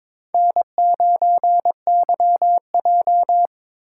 Morse code, heard as a keyed tone, about 23 wpm, 700 Hz. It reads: N9YJ
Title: USA Call Signs - 4 Characters - 4x Character Spacing - 22wpm